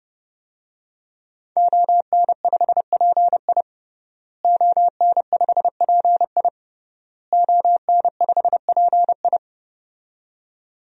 Morse code, heard as a keyed tone, about 30 wpm, 700 Hz. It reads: ON5PI ON5PI ON5PI
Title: DXCC-100 Call Signs - 5 Characters - Repeated 3x - 30wpm